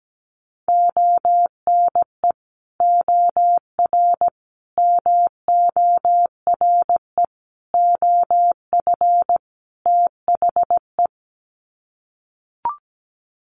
Morse code, heard as a keyed tone, about 17 wpm, 700 Hz. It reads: ONE OR MORE OF THE
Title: Sets of 5 Words Encore 17wpm